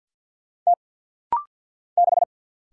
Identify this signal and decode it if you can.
Morse code, heard as a keyed tone, about 48 wpm, 690 Hz. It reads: T 6
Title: Single Letter-Number - Rapid-Fire 50wpm